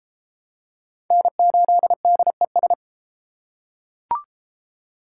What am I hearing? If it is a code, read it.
Morse code, heard as a keyed tone, about 30 wpm, 700 Hz. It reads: N8DES